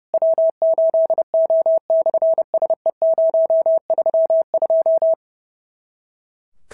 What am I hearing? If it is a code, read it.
Morse code, heard as a keyed tone, about 30 wpm, 650 Hz. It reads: W8O/SE032
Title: SOTA QSO Practice 30wpm